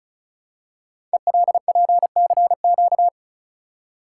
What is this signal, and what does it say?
Morse code, heard as a keyed tone, about 36 wpm, 690 Hz. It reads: ELPCQ
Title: Code Groups - 5 Letters - 35wpm